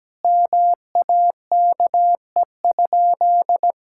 Morse code, heard as a keyed tone, about 17 wpm, 700 Hz. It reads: MAKE?
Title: Sentences from Top 1000 Words - Review Part 5 - 17wpm